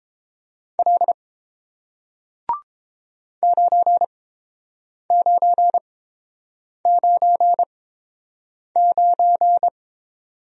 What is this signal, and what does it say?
Morse code, heard as a keyed tone, about 30 wpm, spaced as for 7 wpm, 700 Hz. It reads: L 9999